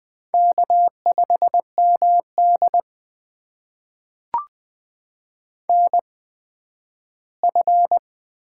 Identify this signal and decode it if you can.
Morse code, heard as a keyed tone, about 21 wpm, spaced as for 12 wpm, 700 Hz. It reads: K5MD NF
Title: USA Call Signs - 4 Characters - 8x Character Spacing - 20wpm